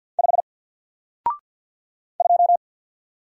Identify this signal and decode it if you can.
Morse code, heard as a keyed tone, about 48 wpm, 700 Hz. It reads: 5 2
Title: Single Character (Letter:Number:Punctuation:Prosign) Alt 50wpm